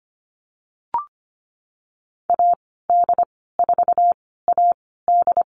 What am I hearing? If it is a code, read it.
Morse code, heard as a keyed tone, about 25 wpm, spaced as for 16 wpm, 700 Hz. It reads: AD4AD
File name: Lesson 15g - Number 5 - Call Signs 25wpm